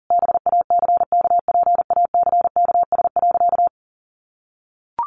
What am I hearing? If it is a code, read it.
Morse code, heard as a keyed tone, about 37 wpm, 700 Hz. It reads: BACKPACKS.
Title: Additional Easy Phrases 40wpm